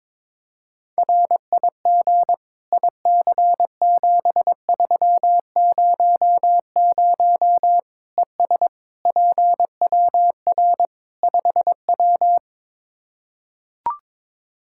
Morse code, heard as a keyed tone, about 22 wpm, 700 Hz. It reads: RIG IC7300 ES PWR 5W
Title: K1USN SST QSO Practice 22wpm